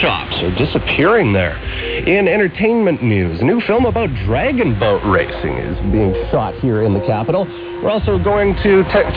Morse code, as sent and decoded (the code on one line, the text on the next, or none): none